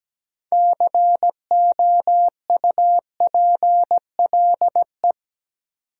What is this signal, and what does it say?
Morse code, heard as a keyed tone, about 17 wpm, 700 Hz. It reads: COUPLE
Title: Sets of 2 Words - New Word Drill - 17wpm